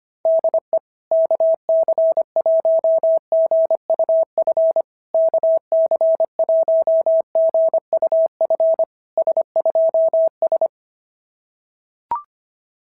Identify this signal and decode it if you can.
Morse code, heard as a keyed tone, about 23 wpm, 650 Hz. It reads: DE KC1GUF KC1GUF S2S
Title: SOTA QSO Practice 25wpm